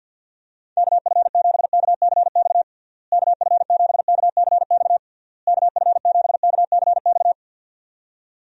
Morse code, heard as a keyed tone, about 48 wpm, 700 Hz. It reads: KF7KCX KF7KCX KF7KCX
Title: USA Call Signs - 6 Characters - Repeated 3x - 50wpm